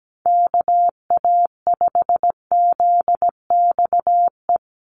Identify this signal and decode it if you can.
Morse code, heard as a keyed tone, about 17 wpm, 700 Hz. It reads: KA5ZXE
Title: USA Call Signs 17wpm